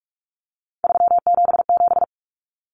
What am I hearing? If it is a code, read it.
Morse code, heard as a keyed tone, about 44 wpm, 700 Hz. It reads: 377
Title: Common Cut Numbers - 3-Digit 45wpm